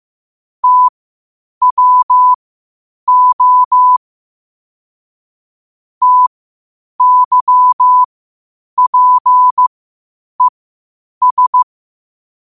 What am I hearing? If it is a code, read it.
Morse code, heard as a keyed tone, about 15 wpm, spaced as for 8 wpm, 1000 Hz. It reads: TWO TYPES
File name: qotdfiles / 8wpm